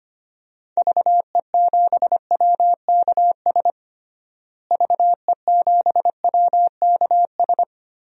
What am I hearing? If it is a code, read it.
Morse code, heard as a keyed tone, about 25 wpm, 700 Hz. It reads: VE7WKS VE7WKS
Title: DXCC-100 Call Signs - 6 Characters - Repeated 3x - 25wpm